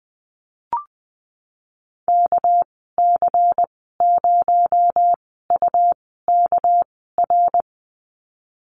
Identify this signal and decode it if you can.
Morse code, heard as a keyed tone, about 20 wpm, spaced as for 14 wpm, 700 Hz. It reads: KC0UKR
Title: 219 Lesson 35g - Number 0 - Call Signs 20wpm